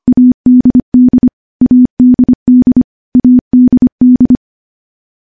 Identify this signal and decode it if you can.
Morse code, heard as a keyed tone, about 25 wpm, 250 Hz. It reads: ADD ADD ADD
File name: morse-training-25WPM